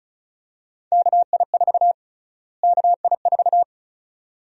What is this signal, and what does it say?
Morse code, heard as a keyed tone, about 36 wpm, 700 Hz. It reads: KI4 KI4
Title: USA Call Sign Prefixes - Repeated 2x - 35wpm